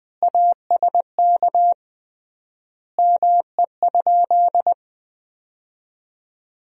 Morse code, heard as a keyed tone, about 20 wpm, 700 Hz. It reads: ASK ME?
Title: 3x Word-Spacing - Sentences from Top 200 Words - 20wpm